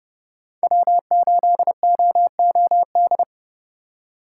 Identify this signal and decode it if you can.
Morse code, heard as a keyed tone, about 30 wpm, 700 Hz. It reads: W8OOD
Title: USA Call Signs - 5 Characters - 4x Character Spacing - 30wpm